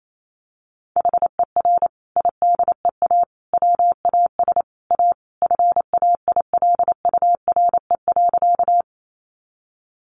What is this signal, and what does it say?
Morse code, heard as a keyed tone, about 28 wpm, 700 Hz. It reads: HER IDEA WAS A FAILURE.
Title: Sentences from Top 900 Words 28wpm